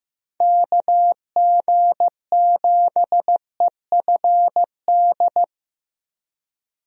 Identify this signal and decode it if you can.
Morse code, heard as a keyed tone, about 15 wpm, 700 Hz. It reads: KG7EFD
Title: USA Call Signs - 6 Characters - Repeated 3x - 15wpm